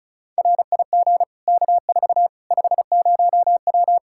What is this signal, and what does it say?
Morse code, heard as a keyed tone, about 34 wpm, 700 Hz. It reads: RIG K4 50W
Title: Practice QSO - Exchange 2 - Info - 35wpm